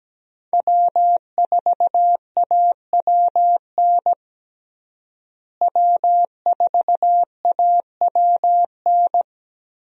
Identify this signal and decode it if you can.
Morse code, heard as a keyed tone, about 17 wpm, 700 Hz. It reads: W4AWN W4AWN
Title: USA Call Signs - 5 Characters - Repeated 3x - 17wpm